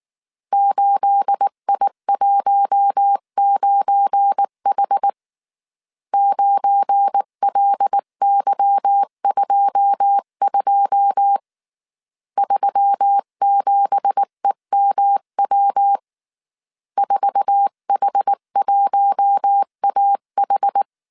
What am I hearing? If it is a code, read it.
Morse code, heard as a keyed tone, about 19 wpm, 800 Hz. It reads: 8I19H 9LY22 37EMW 4H1AH
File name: CW / DET / 1 9 1500C S C 18wpm 2539